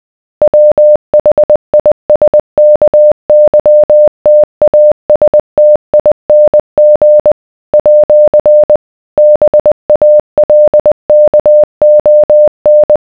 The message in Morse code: WHISKYTASTING @ BALKON